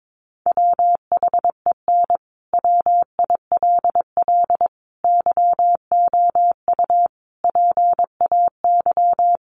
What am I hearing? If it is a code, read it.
Morse code, heard as a keyed tone, about 22 wpm, 700 Hz. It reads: WHEN WILL YOU PAY
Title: Sentences from Top 700 Words 22wpm